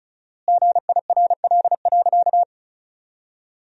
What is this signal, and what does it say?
Morse code, heard as a keyed tone, about 36 wpm, 700 Hz. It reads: GIRL.
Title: 4x Word-Spacing - Sentences from Top 400 Words - 35wpm